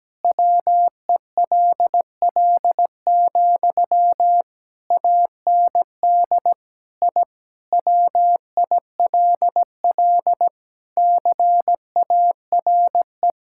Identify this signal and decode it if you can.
Morse code, heard as a keyed tone, about 17 wpm, 700 Hz. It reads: WELL, AND I WILL CARE
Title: Sentences from Top 1000 Words - Review Part 6 - 17wpm